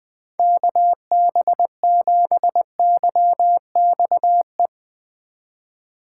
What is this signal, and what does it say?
Morse code, heard as a keyed tone, about 20 wpm, 700 Hz. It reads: KB7YXE
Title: USA Call Signs - 6x Character Spacing - 20wpm